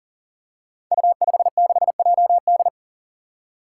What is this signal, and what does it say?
Morse code, heard as a keyed tone, about 39 wpm, 700 Hz. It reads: U56JD
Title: Five Challenging Letter-Number Combinations 40wpm